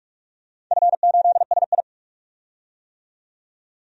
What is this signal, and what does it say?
Morse code, heard as a keyed tone, about 45 wpm, 700 Hz. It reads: F8SI
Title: DXCC-100 Call Signs - 4 Characters - 45wpm